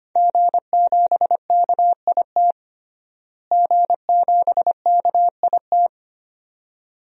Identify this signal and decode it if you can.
Morse code, heard as a keyed tone, about 25 wpm, 700 Hz. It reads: G7KIT G7KIT
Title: DXCC-100 Call Signs - 5 Characters - Repeated 2x - 25wpm